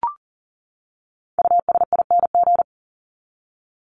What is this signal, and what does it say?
Morse code, heard as a keyed tone, about 40 wpm, 700 Hz. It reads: USING
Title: ING Suffix - Top 100 Words with Base Words 40wpm